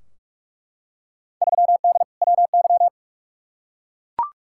Morse code, heard as a keyed tone, about 42 wpm, 700 Hz. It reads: 3D WY
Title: Field Day Exchanges Encore - 45wpm